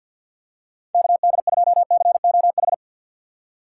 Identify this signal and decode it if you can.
Morse code, heard as a keyed tone, about 52 wpm, 700 Hz. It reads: KD2COH
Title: USA Call Signs - 2x Character Spacing - 50wpm